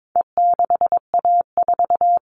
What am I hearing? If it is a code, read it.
Morse code, heard as a keyed tone, about 22 wpm, 700 Hz. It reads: E6A4
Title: All Cut Numbers - 4-Digit 22wpm